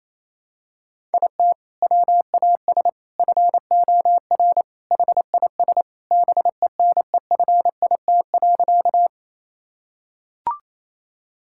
Morse code, heard as a keyed tone, about 28 wpm, 700 Hz. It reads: IT WAS FOR HIS BENEFIT.
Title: Sentences from Top 1000 Words - Review Part 6 - 28wpm